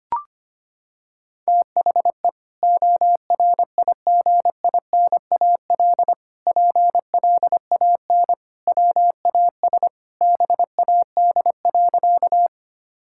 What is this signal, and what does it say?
Morse code, heard as a keyed tone, about 25 wpm, 700 Hz. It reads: THE ORIGINAL PLAN WAS BAD.
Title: Sentences from Top 1000 Words - Review Part 6 - 25wpm